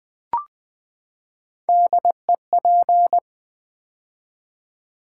Morse code, heard as a keyed tone, about 20 wpm, 700 Hz. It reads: DEP
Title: Build A Word - 3 Letters At A Time - 6-Letter Words - 20wpm